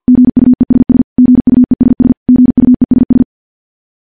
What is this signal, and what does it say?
Morse code, heard as a keyed tone, about 50 wpm, 250 Hz. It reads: GUESS GUESS GUESS
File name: morse / morse-training-50WPM